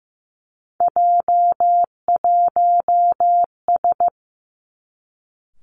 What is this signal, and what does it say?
Morse code, heard as a keyed tone, about 15 wpm, 700 Hz. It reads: J1S
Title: Three Challenging Letter-Number Combinations 15wpm